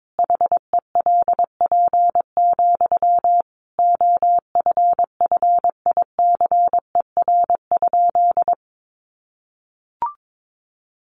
Morse code, heard as a keyed tone, about 22 wpm, 700 Hz. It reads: HELP, OFFICER?
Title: Sentences from Top 1200 Words 22wpm